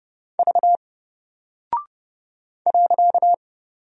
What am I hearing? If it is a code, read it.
Morse code, heard as a keyed tone, about 30 wpm, 700 Hz. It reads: V .
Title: Single Character (Letter:Number:Punctuation:Prosign) Alt 30wpm